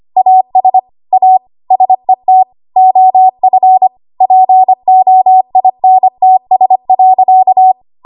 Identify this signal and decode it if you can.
Morse code, heard as a keyed tone, about 25 wpm, 750 Hz. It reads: AS A SET OF POINTS.
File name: Archive / 25 WPM / 25WPM